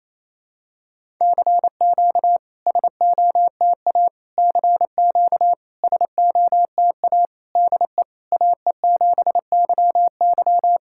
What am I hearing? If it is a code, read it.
Morse code, heard as a keyed tone, about 28 wpm, 690 Hz. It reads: CQ SOTA CQ SOTA DE AE7YY
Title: SOTA QSO Practice 28wpm